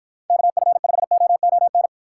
Morse code, heard as a keyed tone, about 48 wpm, 700 Hz. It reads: KF5OON